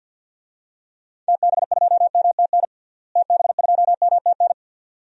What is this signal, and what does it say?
Morse code, heard as a keyed tone, about 50 wpm, 700 Hz. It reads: TB2MTN TB2MTN
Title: DXCC-100 Call Signs - 6 Characters - Repeated 2x - 50wpm